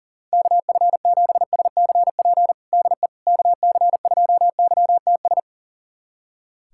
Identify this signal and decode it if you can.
Morse code, heard as a keyed tone, about 40 wpm, 700 Hz. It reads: KF7SCP DE KC2YTS
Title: USA Callsign DE Callsign - 6 Characters - 40wpm